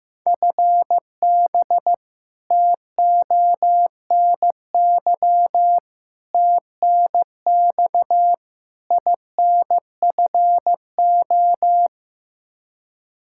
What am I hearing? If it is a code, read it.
Morse code, heard as a keyed tone, about 15 wpm, 700 Hz. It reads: FB TONY TNX INFO
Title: Practice QSO - Exchange 3 - More Info - 15wpm